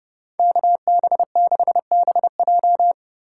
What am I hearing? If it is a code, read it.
Morse code, heard as a keyed tone, about 31 wpm, 700 Hz. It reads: KB6BJ
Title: USA Call Signs - 5 Characters - 4x Character Spacing - 30wpm